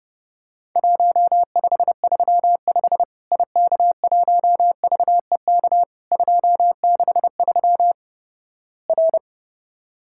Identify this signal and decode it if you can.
Morse code, heard as a keyed tone, about 32 wpm, 700 Hz. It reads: 1535 IK1VEK 263 R
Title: Worked All Europe DX Contest - QTC Exchange - 30wpm